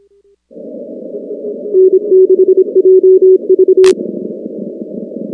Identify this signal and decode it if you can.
Morse code, heard as a keyed tone, about 26 wpm, 380 Hz. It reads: N6JV
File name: contests / 2019-cw-wpx-kt5j-k5pi- -2330 0-19